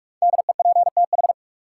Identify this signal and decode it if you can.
Morse code, heard as a keyed tone, about 46 wpm, 700 Hz. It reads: DEPTH
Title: Sentences from Top 1200 Words - New Word Drill Alt - 45wpm